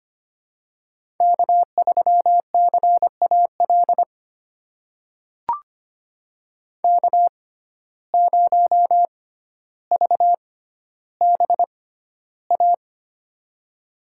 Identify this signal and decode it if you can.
Morse code, heard as a keyed tone, about 25 wpm, spaced as for 13 wpm, 700 Hz. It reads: K3CAL K0VBA
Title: USA Call Signs - 5 Characters - 6x Character Spacing - 25wpm